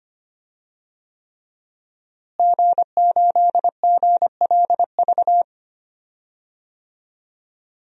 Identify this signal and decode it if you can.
Morse code, heard as a keyed tone, about 25 wpm, 700 Hz. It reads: G8GLV
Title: DXCC-100 Call Signs - 5 Characters - Repeated 3x - 25wpm